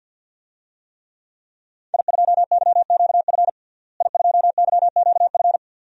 Morse code, heard as a keyed tone, about 45 wpm, 700 Hz. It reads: I2YQF I2YQF
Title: DXCC-100 Call Signs - 5 Characters - Repeated 3x - 50wpm